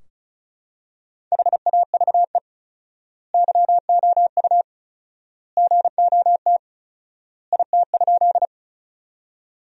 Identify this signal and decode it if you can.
Morse code, heard as a keyed tone, about 34 wpm, 700 Hz. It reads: HAVE YOU GOT IT?
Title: 4x Word-Spacing - Sentences from Top 300 Words - 35wpm